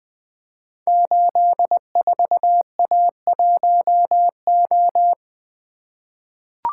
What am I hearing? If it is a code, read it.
Morse code, heard as a keyed tone, about 20 wpm, 700 Hz. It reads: 84A1O
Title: Code Groups - 5 Letter:Numbers - 20wpm